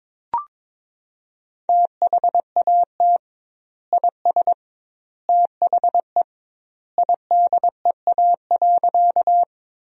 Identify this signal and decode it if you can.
Morse code, heard as a keyed tone, about 22 wpm, 700 Hz. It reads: THAT IS THE IDEA.
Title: 2x Word-Spacing - Sentences from Top 300 Words - 22wpm